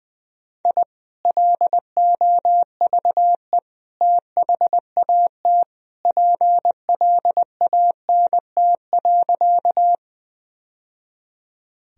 Morse code, heard as a keyed tone, about 20 wpm, 700 Hz. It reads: I LOVE THAT PLANT.
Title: Sentences from Top 1000 Words - Review Part 1 - 20wpm